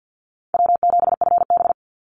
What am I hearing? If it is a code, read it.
Morse code, heard as a keyed tone, about 50 wpm, 700 Hz. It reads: W7FB
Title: Easy Call Signs - 50wpm